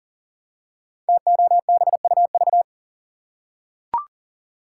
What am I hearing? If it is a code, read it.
Morse code, heard as a keyed tone, about 41 wpm, 700 Hz. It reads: TOBUV
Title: Code Groups - 5 Letters - 40wpm